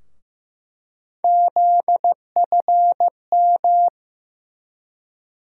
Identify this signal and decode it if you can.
Morse code, heard as a keyed tone, about 15 wpm, 700 Hz. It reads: ZFM
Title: USA Call Sign Postfixes - 3 Letters 15wpm